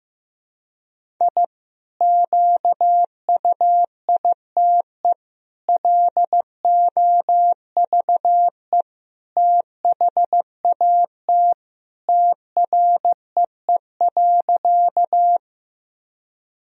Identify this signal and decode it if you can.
Morse code, heard as a keyed tone, about 15 wpm, 700 Hz. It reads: I QUITE LOVE THAT TREE.
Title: Sentences from Top 1000 Words - Review Part 1 - 15wpm